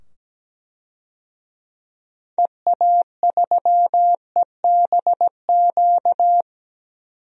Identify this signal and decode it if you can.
Morse code, heard as a keyed tone, about 17 wpm, 700 Hz. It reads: EA3EBQ
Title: DXCC-100 Call Signs - 6 Characters - 17wpm